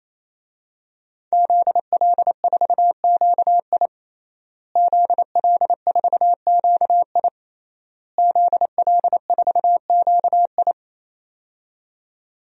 Morse code, heard as a keyed tone, about 28 wpm, 700 Hz. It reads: ZL4QI ZL4QI ZL4QI
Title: DXCC-100 Call Signs - 5 Characters - Repeated 3x - 28wpm